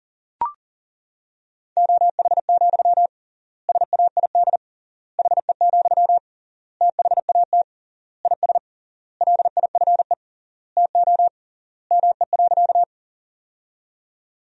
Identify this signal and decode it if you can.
Morse code, heard as a keyed tone, about 37 wpm, 700 Hz. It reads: OH, SAID HE, THAT IS LIFE TO ME.
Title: 3x Word-Spacing - Sentences from Top 500 Words - 40wpm